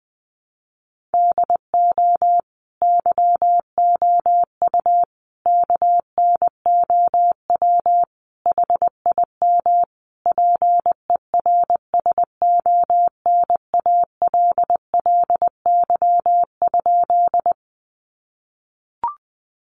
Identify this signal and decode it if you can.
Morse code, heard as a keyed tone, about 20 wpm, 700 Hz. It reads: DO YOU KNOW HIM PERSONALLY?
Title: Sentences from Top 1400 Words 20wpm